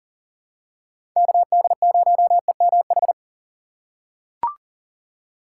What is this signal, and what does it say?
Morse code, heard as a keyed tone, about 40 wpm, 700 Hz. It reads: KD0EMH
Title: USA Call Signs - 6 Characters - 3x Character Spacing - 40wpm